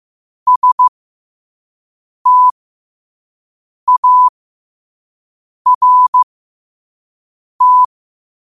Morse code, heard as a keyed tone, about 15 wpm, spaced as for 5 wpm, 1000 Hz. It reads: START
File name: qotdfiles / 5wpm